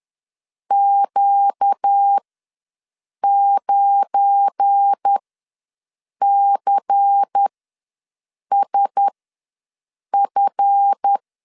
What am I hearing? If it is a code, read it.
Morse code, heard as a keyed tone, about 10 wpm, spaced as for 6 wpm, 800 Hz. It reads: Q9CSF
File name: LEZIONI / 7 GF9Z / 7 GF9 ws 10wpm 4034